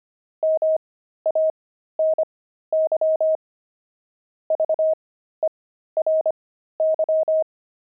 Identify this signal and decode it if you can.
Morse code, heard as a keyed tone, about 25 wpm, spaced as for 13 wpm, 630 Hz. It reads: MANY VERY
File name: pangram-fs-13-cs-25-opal